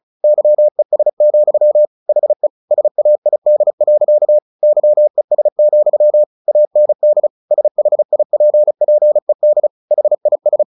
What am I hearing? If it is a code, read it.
Morse code, heard as a keyed tone, about 34 wpm, 600 Hz. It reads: YES, HE SAID. YES, AND SHIPPED HIS
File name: the old man and the sea / the old man and the sea 1 35wpm